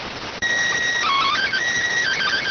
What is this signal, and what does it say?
SSTV (slow-scan television), PD 290.